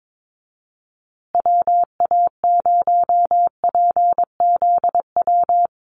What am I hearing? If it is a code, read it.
Morse code, heard as a keyed tone, about 22 wpm, 700 Hz. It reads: WA0PZW
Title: USA Call Signs Repeated 3x 22wpm